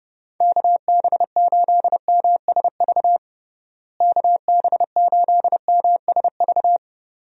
Morse code, heard as a keyed tone, about 30 wpm, 700 Hz. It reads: KB8MSV KB8MSV
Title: USA Call Signs - 6 Characters - Repeated 2x - 30wpm